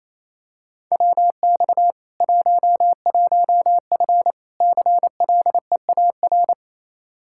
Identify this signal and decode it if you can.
Morse code, heard as a keyed tone, about 28 wpm, 700 Hz. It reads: WX 11F CLEAR
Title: Practice QSO - Exchange 2 - Info - 28wpm